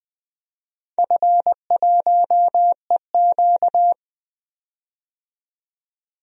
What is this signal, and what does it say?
Morse code, heard as a keyed tone, about 20 wpm, 700 Hz. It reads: F1EQ
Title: DXCC-100 Call Signs - 4 Characters - 20wpm